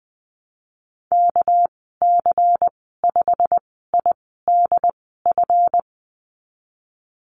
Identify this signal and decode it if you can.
Morse code, heard as a keyed tone, about 20 wpm, spaced as for 15 wpm, 700 Hz. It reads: KC5IDF
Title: 238 Lesson 38g - Letter q - Call Signs 20wpm